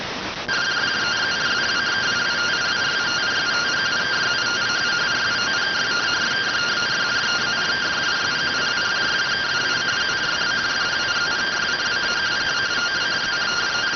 Two-tone teleprinter signal, about 170 Hz shift.